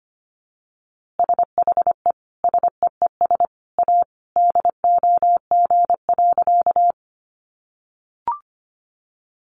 Morse code, heard as a keyed tone, about 25 wpm, 700 Hz. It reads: SHE SEES A DOG.
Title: Easy Phrases 25wpm